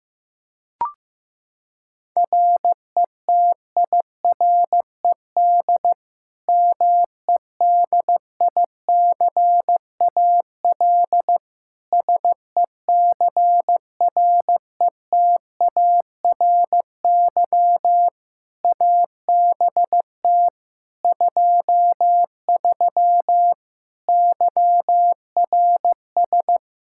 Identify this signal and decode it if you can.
Morse code, heard as a keyed tone, about 16 wpm, 700 Hz. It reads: RETIRED MEDICAL SECRETARY ABT 23 YRS
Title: K1USN SST QSO Practice 15wpm